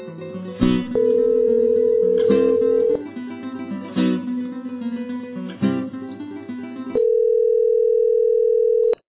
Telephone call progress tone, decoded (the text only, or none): ringback tone